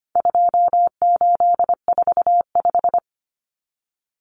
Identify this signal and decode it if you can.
Morse code, heard as a keyed tone, about 25 wpm, 700 Hz. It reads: 2845
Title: Numbers - 4-Digit 25wpm